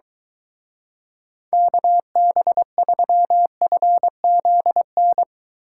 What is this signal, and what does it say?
Morse code, heard as a keyed tone, about 23 wpm, 700 Hz. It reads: KB3FZN